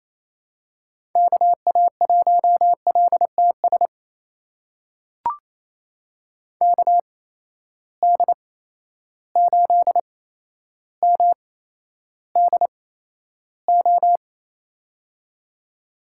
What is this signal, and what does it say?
Morse code, heard as a keyed tone, about 27 wpm, spaced as for 12 wpm, 700 Hz. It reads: KA1LTS KD8MDO
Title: USA Call Signs - 6 Characters - 8x Character Spacing - 28wpm